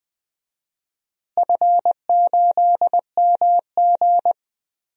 Morse code, heard as a keyed tone, about 20 wpm, 700 Hz. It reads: F8MG
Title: DXCC-100 Call Signs - 4 Characters - Repeated 2x - 20wpm